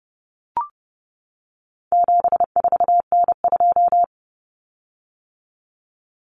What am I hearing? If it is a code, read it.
Morse code, heard as a keyed tone, about 30 wpm, 700 Hz. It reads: 74N2
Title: Common Cut Numbers - 4-Digit 30wpm